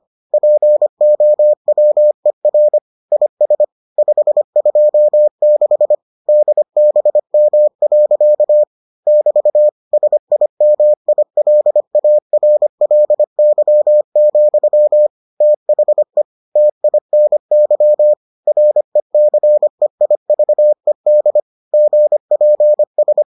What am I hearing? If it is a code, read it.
Morse code, heard as a keyed tone, about 25 wpm, 600 Hz. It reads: POWER IS 526 DBM. <BT> SIMILARLY, THE TINY RECEIVED GPS